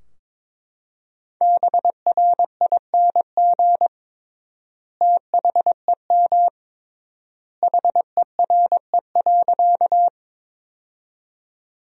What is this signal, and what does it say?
Morse code, heard as a keyed tone, about 22 wpm, 700 Hz. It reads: BRING THEM HERE.
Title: 3x Word-Spacing - Sentences from Top 500 Words - 22wpm